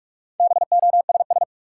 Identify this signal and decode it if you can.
Morse code, heard as a keyed tone, about 46 wpm, 700 Hz. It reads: BOSS
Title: Sentences from Top 700 Words - New Word Drill - 45wpm